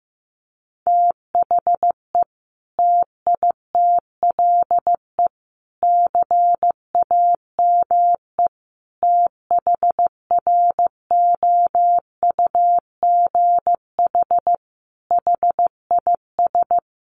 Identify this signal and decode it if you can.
Morse code, heard as a keyed tone, about 15 wpm, 700 Hz. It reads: THE TITLE CAME THROUGH HIS
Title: Sentences from Top 1200 Words 15wpm